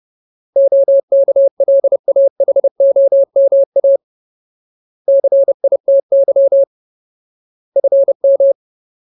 Morse code, heard as a keyed tone, about 30 wpm, 550 Hz. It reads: OKLAHOMA CITY FM
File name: Ultimate Word List / 30WPM Ultimate